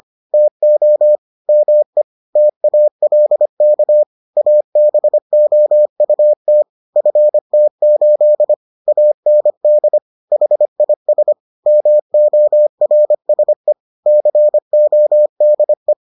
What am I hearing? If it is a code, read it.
Morse code, heard as a keyed tone, about 25 wpm, 600 Hz. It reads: TO ME TALK ABOUT FT8 AND HIS MORSE CODE